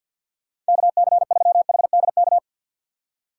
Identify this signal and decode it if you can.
Morse code, heard as a keyed tone, about 53 wpm, 700 Hz. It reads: KC3HDK
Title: USA Call Signs - 6 Characters - 8x Character Spacing - 50wpm